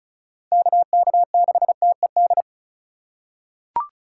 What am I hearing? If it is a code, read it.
Morse code, heard as a keyed tone, about 36 wpm, 700 Hz. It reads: KK6TED
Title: USA Call Signs - 6 Characters - 2x Character Spacing - 35wpm